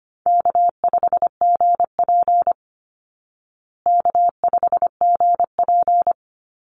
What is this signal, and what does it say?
Morse code, heard as a keyed tone, about 25 wpm, 700 Hz. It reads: K5GP K5GP
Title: Easy Call Signs Repeated 3x - 25wpm